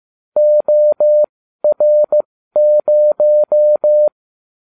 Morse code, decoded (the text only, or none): OR0